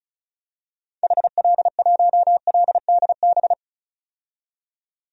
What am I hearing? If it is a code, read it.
Morse code, heard as a keyed tone, about 34 wpm, 700 Hz. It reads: HL1LDB
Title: DXCC-100 Call Signs - 6 Characters - 35wpm